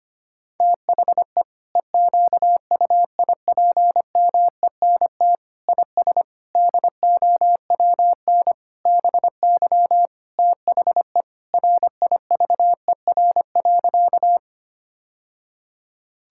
Morse code, heard as a keyed tone, about 25 wpm, 700 Hz. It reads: THE EQUIPMENT IS DOWN BY THE RIVER.
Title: Sentences from Top 1000 Words - Review Part 5 - 25wpm